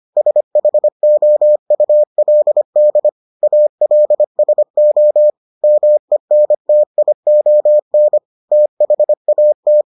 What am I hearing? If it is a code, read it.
Morse code, heard as a keyed tone, about 25 wpm, 600 Hz. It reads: SHOULD ALSO MENTION THAT